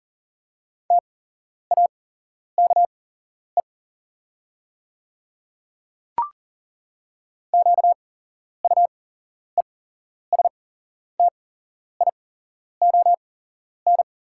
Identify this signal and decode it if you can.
Morse code, heard as a keyed tone, about 40 wpm, spaced as for 11 wpm, 690 Hz. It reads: TAKE QUESTION
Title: 8x Character Spacing - Top 500 Words - 40wpm